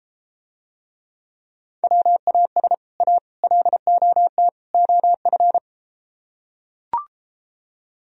Morse code, heard as a keyed tone, about 33 wpm, 700 Hz. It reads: WAS A LOT OF